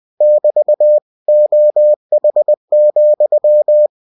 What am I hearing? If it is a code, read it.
Morse code, heard as a keyed tone, about 20 wpm, 600 Hz. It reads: <BT> OH,
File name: cw-books / winnie-the-pooh / winnie-20wpm-page-0003